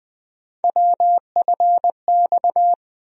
Morse code, heard as a keyed tone, about 20 wpm, 700 Hz. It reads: WFX
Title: USA Call Sign Postfixes - 3 Letters 20wpm